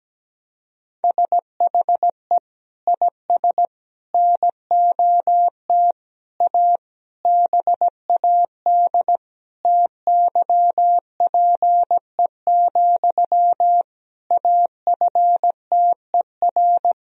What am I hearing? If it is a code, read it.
Morse code, heard as a keyed tone, about 17 wpm, 700 Hz. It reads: SHE IS NOT A BAD TYPE, AFTER
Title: Sentences from Top 1000 Words - Review Part 3 - 17wpm